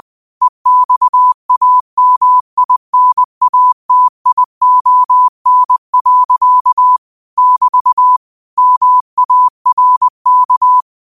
Morse code, heard as a keyed tone, about 20 wpm, 1000 Hz. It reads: EXAMINATION. <BT> MARK